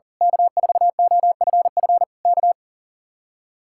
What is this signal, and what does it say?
Morse code, heard as a keyed tone, about 35 wpm, 700 Hz. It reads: K4OFF K
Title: Speed-Racing Common QSO Phrases - 40wpm